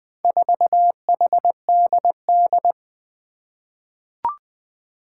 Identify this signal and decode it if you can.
Morse code, heard as a keyed tone, about 20 wpm, 700 Hz. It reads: 4HDD
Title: Four Challenging Letter-Number Combinations 20wpm